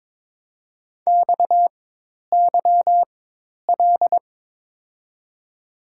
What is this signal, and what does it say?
Morse code, heard as a keyed tone, about 22 wpm, spaced as for 10 wpm, 700 Hz. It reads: XYL
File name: Top 90 QSO Elements - 4x Character Spacing - 22wpm